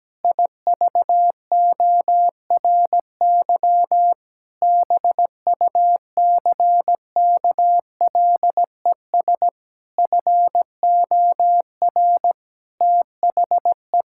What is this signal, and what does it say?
Morse code, heard as a keyed tone, about 17 wpm, 700 Hz. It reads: IVORY BUCKLES FOR THE
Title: Pangram Challenge v2 - 17wpm